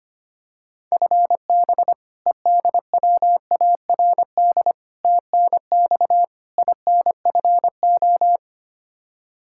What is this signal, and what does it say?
Morse code, heard as a keyed tone, about 25 wpm, 700 Hz. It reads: FB EDWARD TNX INFO
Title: Practice QSO - Exchange 3 - More Info - 25wpm